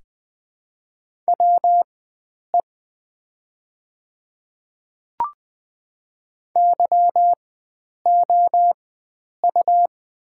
Morse code, heard as a keyed tone, about 20 wpm, spaced as for 9 wpm, 700 Hz. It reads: WE YOU
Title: 4x Character Spacing - Top 100 Words - 20wpm